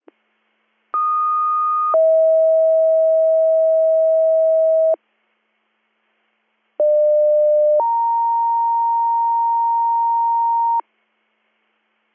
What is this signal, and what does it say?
Two-tone page, heard as a one-second tone, then a three-second tone, about 1230 then 640 Hz, about 600 then 930 Hz. It is heard from 0.9 to 10.8 s.